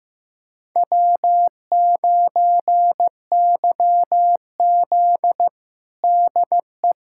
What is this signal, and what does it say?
Morse code, heard as a keyed tone, about 15 wpm, 700 Hz. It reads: W9YZ DE
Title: USA Callsign DE Callsign 15wpm